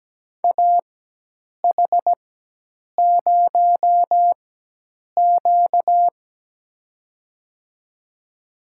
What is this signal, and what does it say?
Morse code, heard as a keyed tone, about 17 wpm, spaced as for 8 wpm, 700 Hz. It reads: AH0Q
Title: USA Call Signs - 4 Characters - 4x Character Spacing - 17wpm